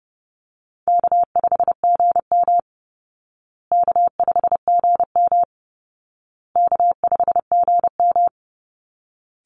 Morse code, heard as a keyed tone, about 30 wpm, 700 Hz. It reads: K5GM K5GM K5GM
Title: Easy Call Signs Repeated 3x - 30wpm